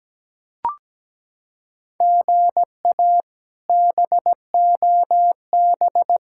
Morse code, heard as a keyed tone, about 17 wpm, 700 Hz. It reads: GA BOB
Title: K1USN SST QSO Practice - 17wpm